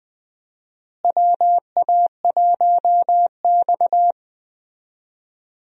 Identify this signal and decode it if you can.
Morse code, heard as a keyed tone, about 20 wpm, 700 Hz. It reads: WA1X
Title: USA Call Signs - 4 Characters - Repeated 2x - 20wpm